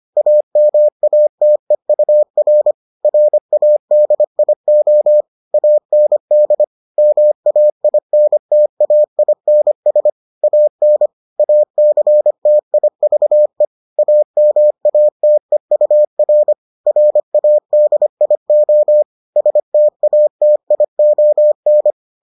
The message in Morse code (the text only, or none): AMATEUR RADIO AND MAINTAINS AN ACTIVE AMATEUR RADIO STATION